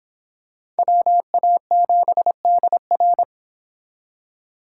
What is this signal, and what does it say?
Morse code, heard as a keyed tone, about 26 wpm, 700 Hz. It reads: WA7DR